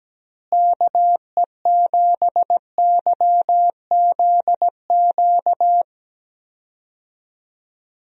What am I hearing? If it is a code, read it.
Morse code, heard as a keyed tone, about 17 wpm, 700 Hz. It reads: KE7YZQ
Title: USA Call Signs - 6 Characters - Repeated 2x - 17wpm